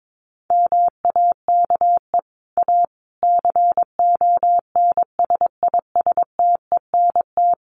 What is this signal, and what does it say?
Morse code, heard as a keyed tone, about 22 wpm, 700 Hz. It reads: MAKE A CONSISTENT
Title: Sentences from Top 1500 Words 22wpm